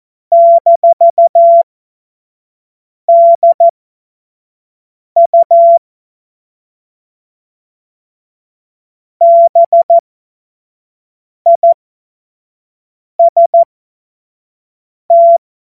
Morse code, heard as a keyed tone, about 14 wpm, spaced as for 5 wpm, 680 Hz. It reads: -DU BIST
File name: Morsekurs / Witze / Witz 10. Mai 2022 25 F70